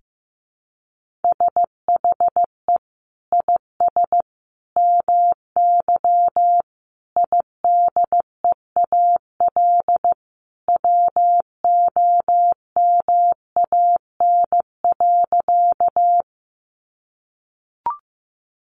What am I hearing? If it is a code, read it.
Morse code, heard as a keyed tone, about 15 wpm, 700 Hz. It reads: SHE IS MY IDEAL WOMAN.
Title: Sentences from Top 1400 Words 15wpm